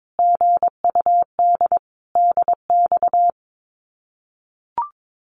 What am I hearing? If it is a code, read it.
Morse code, heard as a keyed tone, about 22 wpm, 700 Hz. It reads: GUD DX
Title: Common QSO Phrases Encore 22wpm